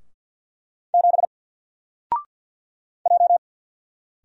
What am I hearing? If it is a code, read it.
Morse code, heard as a keyed tone, about 49 wpm, 700 Hz. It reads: 7 J
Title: Single Letter-Number Alt 50wpm